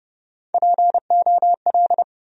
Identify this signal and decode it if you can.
Morse code, heard as a keyed tone, about 30 wpm, 700 Hz. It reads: POL
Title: USA Call Sign Postfixes - 3 Letters 30wpm